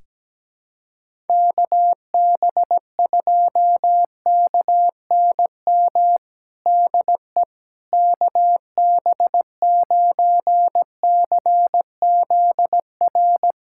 Morse code, heard as a keyed tone, about 17 wpm, 700 Hz. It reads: KB2KNM DE KB9CZR
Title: USA Callsign DE Callsign - 6 Characters - 17wpm